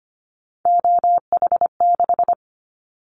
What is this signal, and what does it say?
Morse code, heard as a keyed tone, about 25 wpm, 700 Hz. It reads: OH6
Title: Callsign Prefixes for Top 50 DXCC - 25wpm